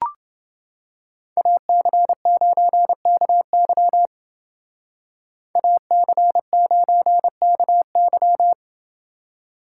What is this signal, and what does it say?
Morse code, heard as a keyed tone, about 29 wpm, 700 Hz. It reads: AC9KY AC9KY